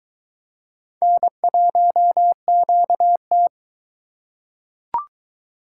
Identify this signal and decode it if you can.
Morse code, heard as a keyed tone, about 22 wpm, 700 Hz. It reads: N1QT